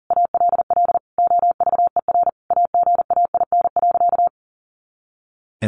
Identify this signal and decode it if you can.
Morse code, heard as a keyed tone, about 40 wpm, 700 Hz. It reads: ALL OVER AGAIN.
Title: Sentences from Top 500 Words 40wpm